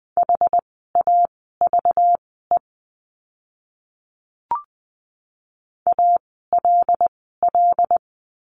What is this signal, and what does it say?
Morse code, heard as a keyed tone, about 20 wpm, spaced as for 14 wpm, 700 Hz. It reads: HAVE ALL
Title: 244 Lesson 39f - Letter z - Words 20wpm